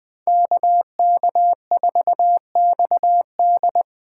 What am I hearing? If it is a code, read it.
Morse code, heard as a keyed tone, about 25 wpm, 700 Hz. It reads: KK4XD
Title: Speed Racing - USA 2x2 Call Signs 20wpm